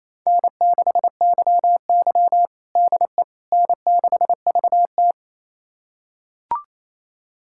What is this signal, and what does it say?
Morse code, heard as a keyed tone, about 28 wpm, 700 Hz. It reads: N6YY DE N6VT
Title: USA Callsign DE Callsign - 4 Characters - 28wpm